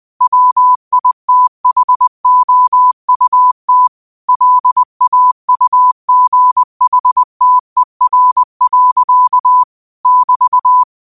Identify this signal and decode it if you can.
Morse code, heard as a keyed tone, about 20 wpm, 1000 Hz. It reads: WITHOUT LAUGHTER. <BT>